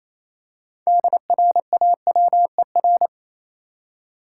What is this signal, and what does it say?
Morse code, heard as a keyed tone, about 27 wpm, 700 Hz. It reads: DRAWER
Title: Six Letter Words - 28wpm